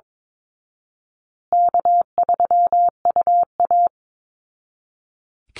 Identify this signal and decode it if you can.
Morse code, heard as a keyed tone, about 22 wpm, 700 Hz. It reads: K3UA
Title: Call Signs Repeated 3x - 22wpm